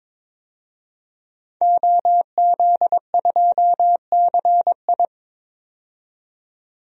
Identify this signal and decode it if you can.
Morse code, heard as a keyed tone, about 22 wpm, 700 Hz. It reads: OZ2CI
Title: DXCC-100 Call Signs - 5 Characters - 22wpm